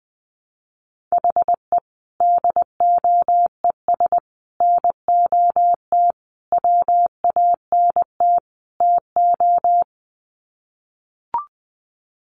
Morse code, heard as a keyed tone, about 20 wpm, 700 Hz. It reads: HE DOES NOT WANT TO
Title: Sets of 5 Words 20wpm